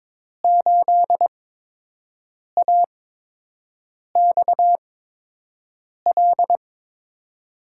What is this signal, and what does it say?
Morse code, heard as a keyed tone, about 21 wpm, spaced as for 6 wpm, 700 Hz. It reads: 8AXL
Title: USA Call Signs - 6 Characters - 8x Character Spacing - 22wpm